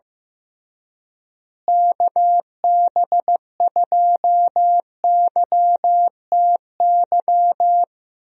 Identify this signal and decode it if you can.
Morse code, heard as a keyed tone, about 15 wpm, 700 Hz. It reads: KB2YTY
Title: USA Call Signs - 6 Characters - Repeated 3x - 15wpm